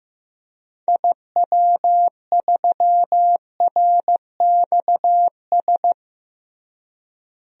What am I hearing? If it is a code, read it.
Morse code, heard as a keyed tone, about 15 wpm, 700 Hz. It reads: IW3RXS
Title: DXCC-100 Call Signs - 6 Characters - Repeated 3x - 15wpm